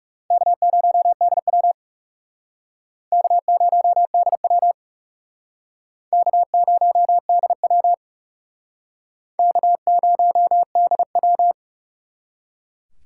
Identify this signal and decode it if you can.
Morse code, heard as a keyed tone, about 37 wpm, 700 Hz. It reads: K0DW K0DW K0DW K0DW
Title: Speed Racing - USA 1x2 Call Signs 30wpm